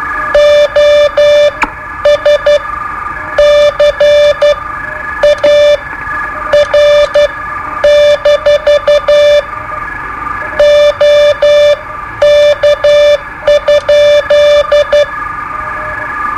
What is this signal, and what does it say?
Morse code, heard as a keyed tone, about 12 wpm, spaced as for 9 wpm, 590 Hz. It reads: OSCAR- OK?
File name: qso-01